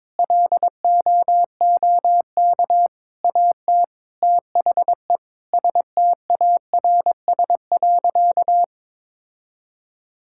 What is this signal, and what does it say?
Morse code, heard as a keyed tone, about 21 wpm, 700 Hz. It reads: LOOK AT THE STARS.
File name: practice / 22wpm / Sentences with 4-Word Phrases - 22wpm